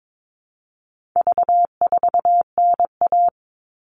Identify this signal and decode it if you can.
Morse code, heard as a keyed tone, about 22 wpm, 700 Hz. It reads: V4NA
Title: All Cut Numbers - 4-Digit 22wpm